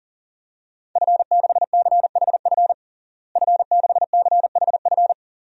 Morse code, heard as a keyed tone, about 40 wpm, 700 Hz. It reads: F6CHF F6CHF
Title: DXCC-100 Call Signs - 5 Characters - Repeated 3x - 40wpm